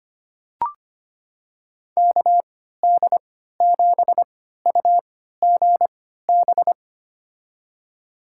Morse code, heard as a keyed tone, about 25 wpm, spaced as for 14 wpm, 700 Hz. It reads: KD7UGB
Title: USA Call Signs - 6 Characters - 3x Character Spacing - 25wpm